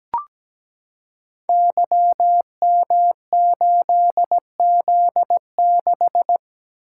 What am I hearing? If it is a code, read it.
Morse code, heard as a keyed tone, about 17 wpm, 700 Hz. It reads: YM8Z6
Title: Code Groups - 5 Letter:Numbers - 17wpm